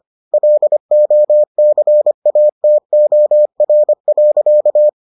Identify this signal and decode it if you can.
Morse code, heard as a keyed tone, about 25 wpm, 600 Hz. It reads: LOCATOR.